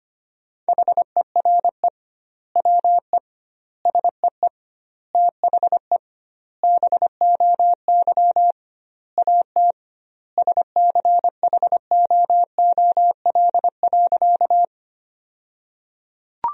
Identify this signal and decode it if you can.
Morse code, heard as a keyed tone, about 25 wpm, 700 Hz. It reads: HERE WE SEE THE BOY AT SCHOOL.
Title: 2x Word-Spacing - Sentences from Top 300 Words - 25wpm